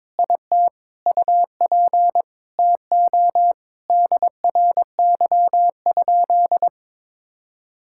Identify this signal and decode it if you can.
Morse code, heard as a keyed tone, about 22 wpm, 700 Hz. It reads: IT UP TO DRY?
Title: Sentences from Top 1600 Words 22wpm